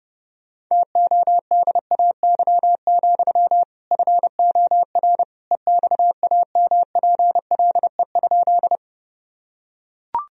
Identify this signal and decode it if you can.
Morse code, heard as a keyed tone, about 30 wpm, 700 Hz. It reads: TODAY, FOR EXAMPLE?
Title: Sentences from Top 1000 Words - Review Part 1 - 30wpm